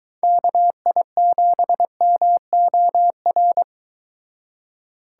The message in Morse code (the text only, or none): KI7MOR